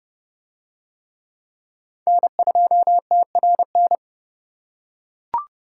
Morse code, heard as a keyed tone, about 30 wpm, 700 Hz. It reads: N2TRN